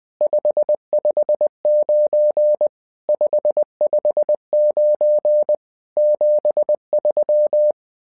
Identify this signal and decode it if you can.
Morse code, heard as a keyed tone, about 20 wpm, 600 Hz. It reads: 559 559 73
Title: SOTA QSO Practice 20wpm